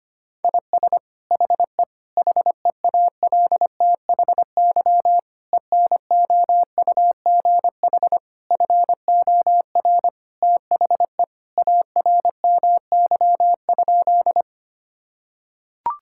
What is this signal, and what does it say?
Morse code, heard as a keyed tone, about 25 wpm, 700 Hz. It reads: IS HE HEALTHY ENOUGH FOR THE ARMY?
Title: Sentences from Top 1000 Words - Review Part 5 - 25wpm